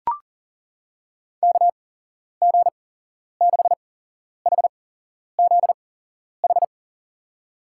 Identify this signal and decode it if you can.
Morse code, heard as a keyed tone, about 40 wpm, spaced as for 11 wpm, 700 Hz. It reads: KG6HZH
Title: USA Call Signs - 6 Characters - 8x Character Spacing - 40wpm